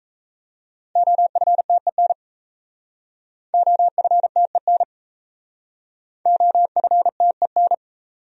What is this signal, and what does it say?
Morse code, heard as a keyed tone, about 35 wpm, 700 Hz. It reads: OFTEN OFTEN OFTEN
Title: Speed-Racing Top 500 Words 28wpm